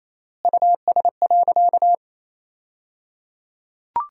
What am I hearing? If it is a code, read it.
Morse code, heard as a keyed tone, about 28 wpm, 700 Hz. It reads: US.
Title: 3x Word-Spacing - Sentences from Top 300 Words - 28wpm